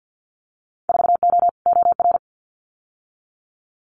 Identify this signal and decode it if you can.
Morse code, heard as a keyed tone, about 50 wpm, 700 Hz. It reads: 4O OR
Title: Winter Field Day Exchanges Encore 50wpm